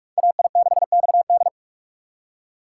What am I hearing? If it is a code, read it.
Morse code, heard as a keyed tone, about 43 wpm, 700 Hz. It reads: AI6XD
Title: USA Call Signs - 5 Characters - 3x Character Spacing - 45wpm